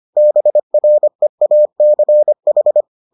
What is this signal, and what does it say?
Morse code, heard as a keyed tone, about 25 wpm, 600 Hz. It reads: BREACH